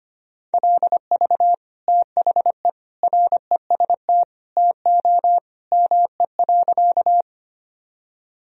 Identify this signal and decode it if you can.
Morse code, heard as a keyed tone, about 25 wpm, 700 Hz. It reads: LV THE REST TO ME.
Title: Sentences from Top 500 Words and Abbreviations 25wpm